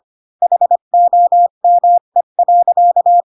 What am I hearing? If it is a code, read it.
Morse code, heard as a keyed tone, about 25 wpm, 700 Hz. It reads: HOME.
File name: Pinocchio / Pinocchio ch3 20 25 ik1hge